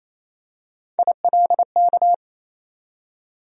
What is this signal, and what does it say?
Morse code, heard as a keyed tone, about 28 wpm, 700 Hz. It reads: ILK
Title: USA Call Sign Postfixes - 3 Letters 28wpm